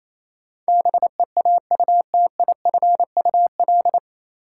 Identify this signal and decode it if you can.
Morse code, heard as a keyed tone, about 28 wpm, 700 Hz. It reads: BEAUTIFUL
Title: Sentences from Top 900 Words - New Word Drill - 28wpm